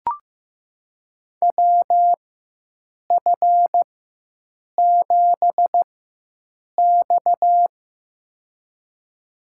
Morse code, heard as a keyed tone, about 15 wpm, spaced as for 7 wpm, 700 Hz. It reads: WF7X
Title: USA Call Signs - 4 Characters - 4x Character Spacing - 15wpm